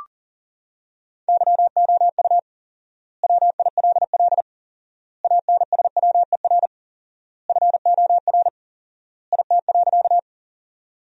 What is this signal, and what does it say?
Morse code, heard as a keyed tone, about 40 wpm, 700 Hz. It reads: YOU WILL ANSWER FOR IT.
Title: 4x Word-Spacing - Sentences from Top 300 Words - 40wpm